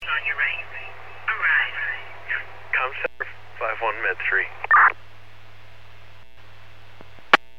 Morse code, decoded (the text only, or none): none